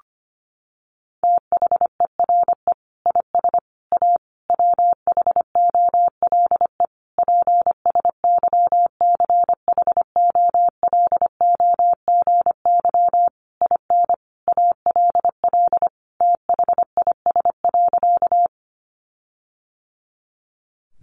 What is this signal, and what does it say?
Morse code, heard as a keyed tone, about 25 wpm, 700 Hz. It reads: THERE IS A WHOLE PSYCHOLOGY IN ALL THIS.
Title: Sentences from Top 1400 Words 25wpm